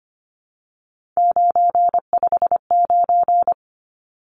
Morse code, heard as a keyed tone, about 25 wpm, 700 Hz. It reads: 959
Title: Numbers - 3-Digit 25wpm